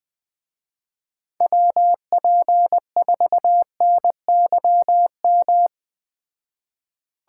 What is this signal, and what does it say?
Morse code, heard as a keyed tone, about 20 wpm, 700 Hz. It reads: WP4NYM
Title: Speed Racing - USA Call Signs 17wpm